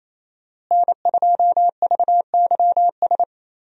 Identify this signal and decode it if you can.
Morse code, heard as a keyed tone, about 28 wpm, 700 Hz. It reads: N2VYS
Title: USA Call Signs - 5 Characters - Repeated 2x - 28wpm